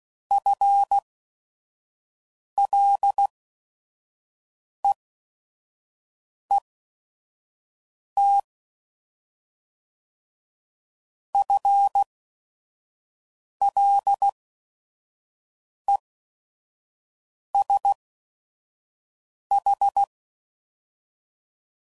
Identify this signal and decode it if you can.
Morse code, heard as a keyed tone, about 16 wpm, spaced as for 5 wpm, 800 Hz. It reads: FLEET FLESH